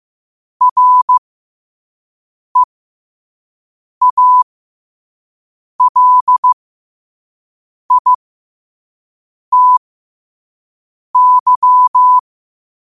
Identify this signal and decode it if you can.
Morse code, heard as a keyed tone, about 15 wpm, spaced as for 5 wpm, 1000 Hz. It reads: REALITY
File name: qotdfiles / 5wpm